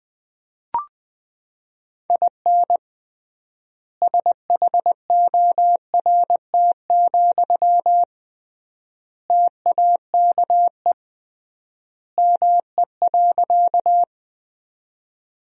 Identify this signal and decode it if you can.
Morse code, heard as a keyed tone, about 20 wpm, 700 Hz. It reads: IN SHORT, TAKE ME.
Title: 3x Word-Spacing - Sentences from Top 400 Words - 20wpm